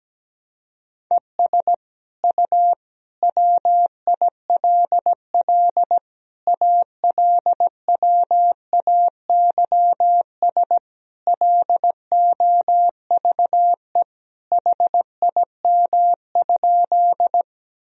Morse code, heard as a keyed tone, about 17 wpm, 700 Hz. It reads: ES U WILL ALWAYS LOVE HIM?
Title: Sentences from Top 400 Words and Abbreviations 17wpm